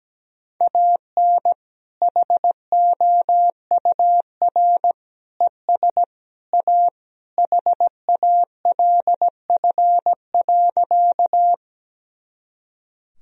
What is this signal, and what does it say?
Morse code, heard as a keyed tone, about 17 wpm, 700 Hz. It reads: AN HOUR ES A HALF.
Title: Sentences from Top 400 Words and Abbreviations 17wpm